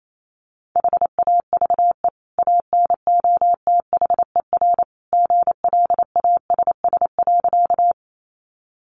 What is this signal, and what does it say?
Morse code, heard as a keyed tone, about 28 wpm, 700 Hz. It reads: HAVE ANOTHER GLASS.
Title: Sentences from Top 1300 Words 28wpm